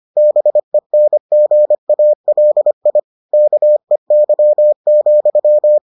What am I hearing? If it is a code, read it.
Morse code, heard as a keyed tone, about 25 wpm, 600 Hz. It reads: BENGALI KEY,